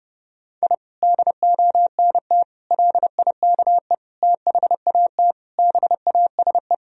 Morse code, heard as a keyed tone, about 30 wpm, 700 Hz. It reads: I DONT LIKE THAT BASE
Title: Sentences from Top 1000 Words - Review Part 1 - 30wpm